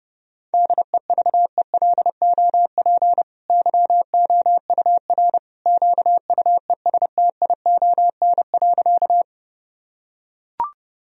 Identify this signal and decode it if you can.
Morse code, heard as a keyed tone, about 30 wpm, 700 Hz. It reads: DEVELOP YOUR QUESTION.
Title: Sentences from Top 1000 Words - Review Part 2 - 30wpm